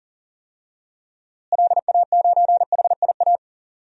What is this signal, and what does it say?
Morse code, heard as a keyed tone, about 40 wpm, 700 Hz. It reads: LA9HIA
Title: DXCC-100 Call Signs - 6 Characters - Repeated 3x - 40wpm